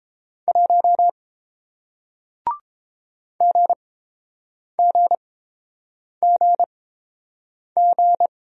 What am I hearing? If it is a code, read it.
Morse code, heard as a keyed tone, about 30 wpm, spaced as for 7 wpm, 700 Hz. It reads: 1 GGGG